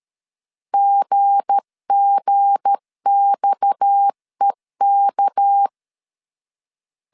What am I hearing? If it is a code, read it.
Morse code, heard as a keyed tone, about 13 wpm, 800 Hz. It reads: GGXEK
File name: LEZIONI / 7 GF9Z / 7 G 12wpm 2055